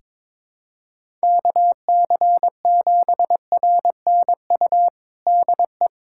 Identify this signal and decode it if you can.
Morse code, heard as a keyed tone, about 22 wpm, 700 Hz. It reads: KC7RNU DE
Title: USA Callsign DE Callsign - 6 Characters - 22wpm